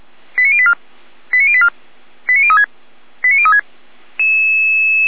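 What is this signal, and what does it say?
Five-tone selective call (ZVEI), heard as quick runs of five tones, tones of about 70 ms. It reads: 89084 89084 89036 89036
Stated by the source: Five-tone ZVEI selective call received by SDR, from feuerwehr-orlamuende.org fire brigade audio server.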